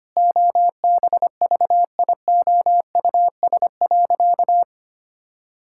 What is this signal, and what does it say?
Morse code, heard as a keyed tone, about 25 wpm, 700 Hz. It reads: OBVIOUS.
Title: Sentences from Top 1600 Words 25wpm